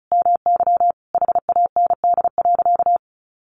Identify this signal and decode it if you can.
Morse code, heard as a keyed tone, about 34 wpm, 700 Hz. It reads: MY HAND.
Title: Sentences from Top 1300 Words 35wpm